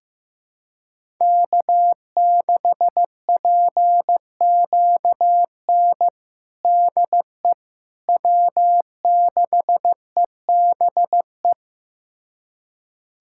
Morse code, heard as a keyed tone, about 16 wpm, 700 Hz. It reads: K6PQN DE W6EBE
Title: USA Callsign DE Callsign - 5 Characters - 15wpm